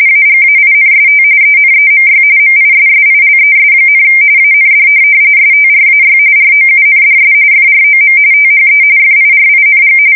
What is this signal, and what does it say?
RTTY, about 45.45 baud. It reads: AGAINST ENEMY MISSILES, AND STRAY ROUNDS FROM THE FIRING
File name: fsk170 / USS-Missouri